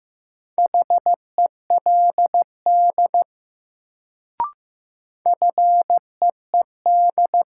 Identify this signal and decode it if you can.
Morse code, heard as a keyed tone, about 15 wpm, 700 Hz. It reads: HELD FEED
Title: Sentences from Top 1600 Words - New Word Drill - 15wpm